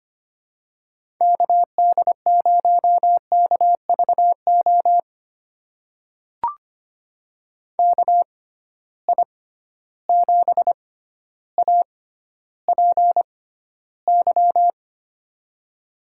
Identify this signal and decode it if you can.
Morse code, heard as a keyed tone, about 25 wpm, spaced as for 13 wpm, 700 Hz. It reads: KD0KVO KI7APY
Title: USA Call Signs - 6 Characters - 6x Character Spacing - 25wpm